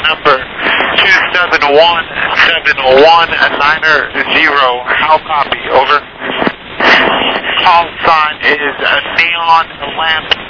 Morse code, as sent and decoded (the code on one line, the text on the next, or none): none